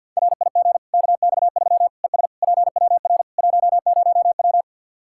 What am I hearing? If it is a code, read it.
Morse code, heard as a keyed tone, about 53 wpm, 700 Hz. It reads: RIG KX3 ES PWR 10W
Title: Practice QSO - Exchange 2 - Info - 50wpm